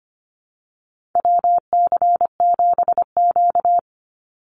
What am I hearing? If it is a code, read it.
Morse code, heard as a keyed tone, about 25 wpm, 700 Hz. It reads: WC7Q
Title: Call Signs Repeated 2x - 25wpm